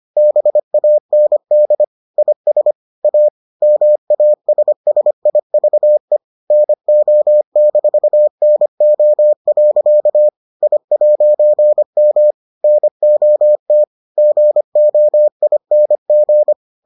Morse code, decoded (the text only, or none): BAND IS A MASSIVE NO-NO. I'M NOT GOING